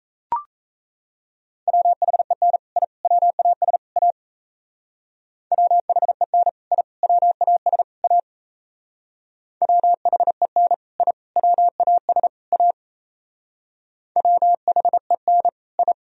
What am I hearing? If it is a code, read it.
Morse code, heard as a keyed tone, about 35 wpm, 700 Hz. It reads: WHEN I WAS A WHEN I WAS A WHEN I WAS A WHEN I
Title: Speed-Racing - Sets of 4 Words 28wpm